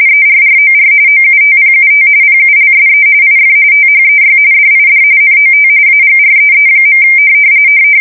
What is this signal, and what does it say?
RTTY, about 45.45 baud. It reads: ERROR OUTPUT PULSE IF THEY DID NOT AGREE.
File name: fsk170 / teletype-corp